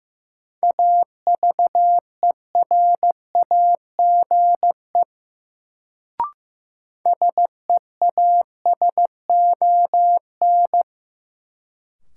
Morse code, heard as a keyed tone, about 15 wpm, 700 Hz. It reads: AVERAGE SEASON
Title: Sentences from Top 1400 Words - New Word Drill Alt - 15wpm